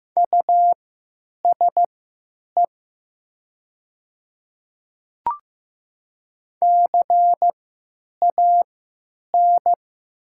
Morse code, heard as a keyed tone, about 15 wpm, spaced as for 8 wpm, 700 Hz. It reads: USE CAN
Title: 3x Character Spacing - Top 200 Words - 15wpm